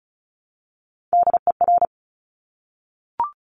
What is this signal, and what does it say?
Morse code, heard as a keyed tone, about 36 wpm, 700 Hz. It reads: DER
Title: Top Three Letter Combinations in Words 35wpm